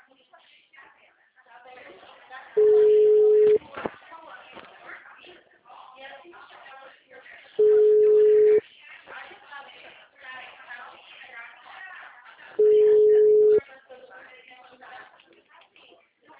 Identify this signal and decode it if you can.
Ringback tone (Europe, 425 Hz), with speech.